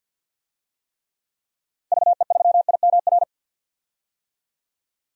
Morse code, heard as a keyed tone, about 52 wpm, 700 Hz. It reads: VE3IMR
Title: DXCC-100 Call Signs - 6 Characters - 50wpm